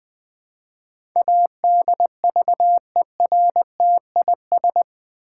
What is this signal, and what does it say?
Morse code, heard as a keyed tone, about 20 wpm, 700 Hz. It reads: ADVERTIS
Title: Build A Word - 4 Letters At A Time - 11-Letter Words - 20wpm